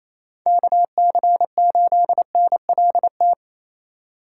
Morse code, heard as a keyed tone, about 27 wpm, 700 Hz. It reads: KC8NLT
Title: USA Call Signs - 6 Characters - 6x Character Spacing - 28wpm